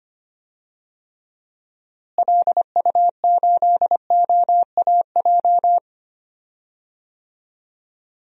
Morse code, heard as a keyed tone, about 25 wpm, 700 Hz. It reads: LU8OAJ
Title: DXCC-100 Call Signs - 6 Characters - Repeated 2x - 25wpm